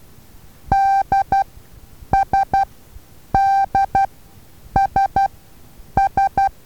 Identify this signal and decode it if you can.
Morse code, heard as a keyed tone, about 12 wpm, spaced as for 8 wpm, 780 Hz. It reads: DSDSS
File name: Taviro HA6VA A